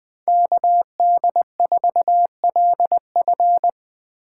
Morse code, heard as a keyed tone, about 22 wpm, 700 Hz. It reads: KD4LF
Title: Speed Racing - USA 2x2 Call Signs 20wpm